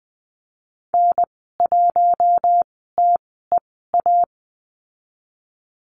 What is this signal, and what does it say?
Morse code, heard as a keyed tone, about 19 wpm, spaced as for 14 wpm, 700 Hz. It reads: N1TEA
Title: 082 Lesson 14g - Number 2 - Call Signs 20wpm